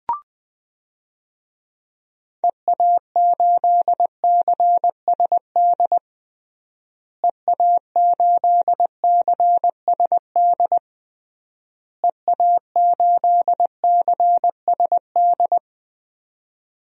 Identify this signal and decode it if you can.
Morse code, heard as a keyed tone, about 20 wpm, 700 Hz. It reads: EA8CSD EA8CSD EA8CSD
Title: DXCC-100 Call Signs - 6 Characters - Repeated 3x - 20wpm